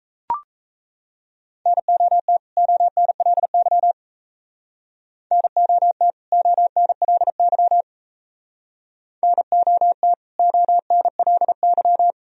Morse code, heard as a keyed tone, about 34 wpm, 700 Hz. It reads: NOT ONLY NOT ONLY NOT ONLY
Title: Speed-Racing - Sets of 2 Words 28wpm